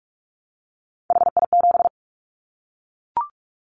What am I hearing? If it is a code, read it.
Morse code, heard as a keyed tone, about 44 wpm, 700 Hz. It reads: HI7
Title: Callsign Prefixes for Top 100 DXCC - 45wpm